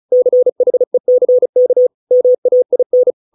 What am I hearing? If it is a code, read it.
Morse code, heard as a keyed tone, about 34 wpm, 500 Hz. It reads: CHECK MAIN